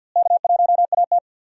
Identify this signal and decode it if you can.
Morse code, heard as a keyed tone, about 52 wpm, 700 Hz. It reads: K1AT
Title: USA Call Signs - 4 Characters - Repeated 3x - 50wpm